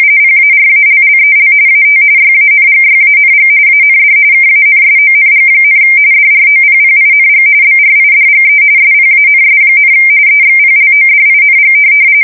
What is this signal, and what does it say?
RTTY, about 45.45 baud. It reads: PARALLEL AT YANGYANG AND KANSONG, DROPPING BRIDGE SPANS, EXPLODING
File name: fsk170 / USS-New-Jersey